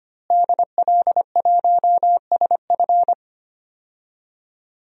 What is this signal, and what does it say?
Morse code, heard as a keyed tone, about 25 wpm, 700 Hz. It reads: DL1SF
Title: DXCC-100 Call Signs - 5 Characters - Repeated 2x - 25wpm